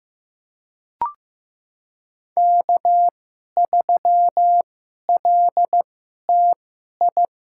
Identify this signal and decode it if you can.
Morse code, heard as a keyed tone, about 15 wpm, spaced as for 11 wpm, 700 Hz. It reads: K3LTI
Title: USA Call Signs - 2x Character Spacing - 15wpm